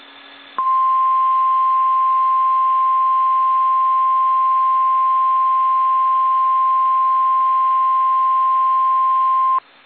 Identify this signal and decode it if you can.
Emergency Alert System (SAME) tones (the 1050 Hz attention tone). It is heard from 0.5 to 9.7 s.
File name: eas / KSDZKDJL / KSDZKDJL